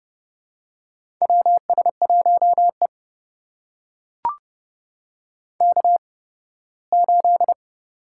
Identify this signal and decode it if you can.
Morse code, heard as a keyed tone, about 30 wpm, spaced as for 18 wpm, 700 Hz. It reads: WS1E K8
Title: USA Call Signs - 4 Characters - 8x Character Spacing - 30wpm